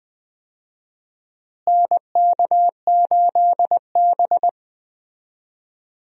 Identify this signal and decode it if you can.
Morse code, heard as a keyed tone, about 20 wpm, 700 Hz. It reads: NK8B
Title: USA Call Signs - 4 Characters - Repeated 2x - 20wpm